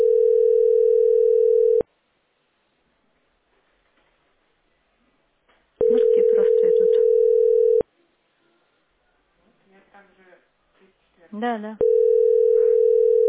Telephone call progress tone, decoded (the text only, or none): ringback tone